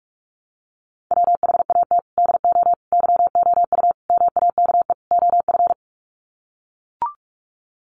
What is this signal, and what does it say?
Morse code, heard as a keyed tone, about 44 wpm, 700 Hz. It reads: WHAT DO YOU MAKE OF
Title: Sets of 5 Words 45wpm